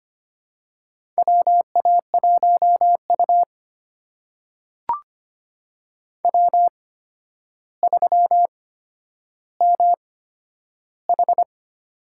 Morse code, heard as a keyed tone, about 25 wpm, spaced as for 11 wpm, 700 Hz. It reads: WA1U W3MH
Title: USA Call Signs - 8x Character Spacing - 25wpm